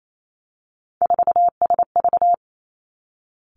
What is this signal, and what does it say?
Morse code, heard as a keyed tone, about 28 wpm, 700 Hz. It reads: 4SV
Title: Three Challenging Letter-Number Combinations 28wpm